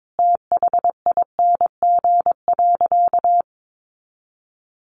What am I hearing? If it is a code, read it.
Morse code, heard as a keyed tone, about 22 wpm, 700 Hz. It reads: THING.
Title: Sentences from Top 700 Words 22wpm